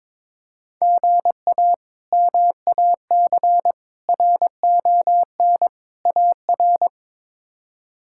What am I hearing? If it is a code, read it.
Morse code, heard as a keyed tone, about 22 wpm, 700 Hz. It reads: GA MAC RON AR
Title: K1USN SST QSO Practice - 22wpm